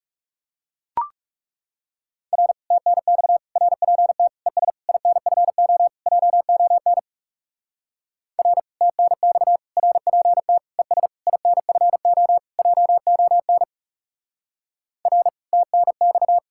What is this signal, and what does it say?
Morse code, heard as a keyed tone, about 36 wpm, 700 Hz. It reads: R TNX RPT ES INFO JON R TNX RPT ES INFO JON R TNX
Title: Speed-Racing Common QSO Phrases - 30wpm